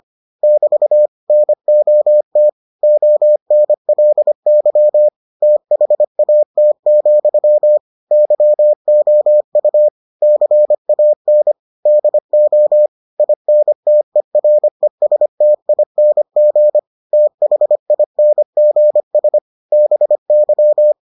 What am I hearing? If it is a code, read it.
Morse code, heard as a keyed tone, about 25 wpm, 600 Hz. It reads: <BT> NOT ONLY THAT, YOU CAN DO INTERESTING THINGS BY